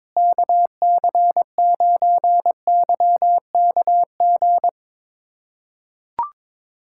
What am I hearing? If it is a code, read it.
Morse code, heard as a keyed tone, about 22 wpm, 700 Hz. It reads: KC9YKG
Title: USA Callsign DE Callsign - 6 Characters - 22wpm